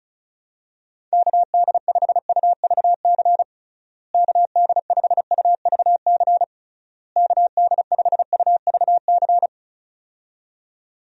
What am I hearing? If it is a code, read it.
Morse code, heard as a keyed tone, about 36 wpm, 700 Hz. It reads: KD5UVC KD5UVC KD5UVC
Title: USA Call Signs - 6 Characters - Repeated 3x - 35wpm